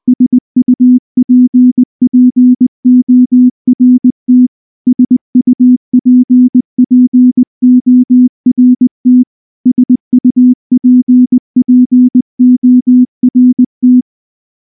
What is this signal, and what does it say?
Morse code, heard as a keyed tone, about 20 wpm, 250 Hz. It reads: SUPPORT SUPPORT SUPPORT
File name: morse / morse-training-20WPM